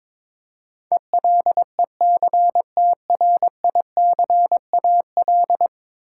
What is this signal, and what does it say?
Morse code, heard as a keyed tone, about 22 wpm, 700 Hz. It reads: ELECTRICAL
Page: morse-code-ninja-practice-22wpm